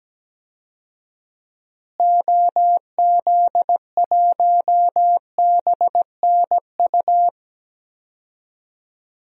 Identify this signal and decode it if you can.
Morse code, heard as a keyed tone, about 17 wpm, 700 Hz. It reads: OZ1BNU
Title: DXCC-100 Call Signs - 6 Characters - Repeated 2x - 17wpm